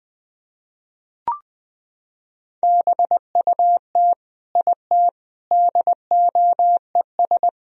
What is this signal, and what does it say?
Morse code, heard as a keyed tone, about 20 wpm, 700 Hz. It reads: BUT IT DOES
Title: Sentences from Top 1000 Words - Review Part 2 - 20wpm